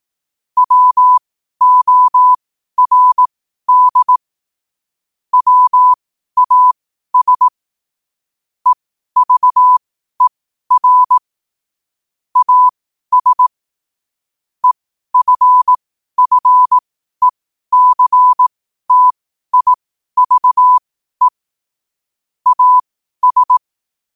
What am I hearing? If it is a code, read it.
Morse code, heard as a keyed tone, about 18 wpm, spaced as for 12 wpm, 1000 Hz. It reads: WORD WAS EVER AS EFFECTIVE AS
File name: qotdfiles / 12wpm